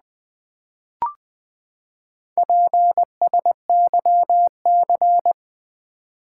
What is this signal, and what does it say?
Morse code, heard as a keyed tone, about 20 wpm, 700 Hz. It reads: PSYC
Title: Build A Word - 4 Letters At A Time - 12-Letter Words - 20wpm